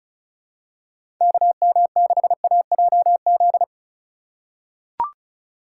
Morse code, heard as a keyed tone, about 36 wpm, 700 Hz. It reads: KM6AJZ
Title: USA Call Signs - 6x Character Spacing - 35wpm